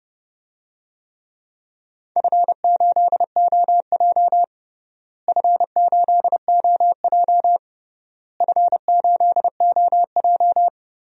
Morse code, heard as a keyed tone, about 32 wpm, 700 Hz. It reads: F8OJ F8OJ F8OJ
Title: DXCC-100 Call Signs - 4 Characters - Repeated 3x - 30wpm